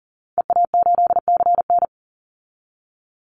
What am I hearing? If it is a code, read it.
Morse code, heard as a keyed tone, about 40 wpm, 700 Hz. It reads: EA8CN
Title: Call Signs Repeated 3x - 40wpm